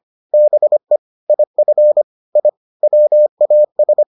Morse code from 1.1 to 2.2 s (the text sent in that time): IF